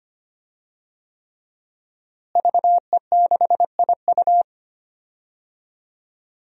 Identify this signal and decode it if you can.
Morse code, heard as a keyed tone, about 25 wpm, 700 Hz. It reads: VE6IU
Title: DXCC-100 Call Signs - 5 Characters - Repeated 3x - 25wpm